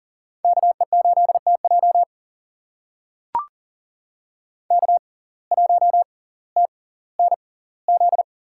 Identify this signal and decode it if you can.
Morse code, heard as a keyed tone, about 40 wpm, spaced as for 20 wpm, 700 Hz. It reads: KE8TJ K1TNZ
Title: USA Call Signs - 5 Characters - 6x Character Spacing - 40wpm